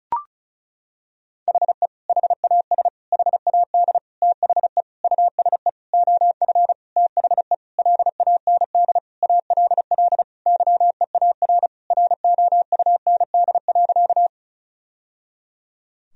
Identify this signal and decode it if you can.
Morse code, heard as a keyed tone, about 34 wpm, 700 Hz. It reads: HE HAS HAD THE USE OF THE LAND ALL YEAR ROUND.
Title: Sentences from Top 1000 Words - Review Part 1 - 35wpm